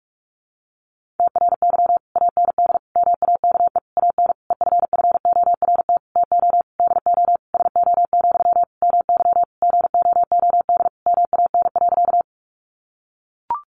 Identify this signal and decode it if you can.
Morse code, heard as a keyed tone, about 44 wpm, 700 Hz. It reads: TRY AND MAKE AN EFFORT TO DO SO, MY GOOD MAN.
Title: Sentences from Top 1200 Words 45wpm